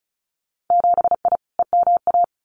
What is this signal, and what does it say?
Morse code, heard as a keyed tone, about 34 wpm, 700 Hz. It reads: 7I EMA
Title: Winter Field Day Exchanges Encore 35wpm